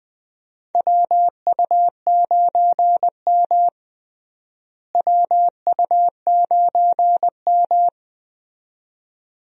Morse code, heard as a keyed tone, about 20 wpm, 700 Hz. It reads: WU9M WU9M
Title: USA Call Signs - 4 Characters - Repeated 2x - 20wpm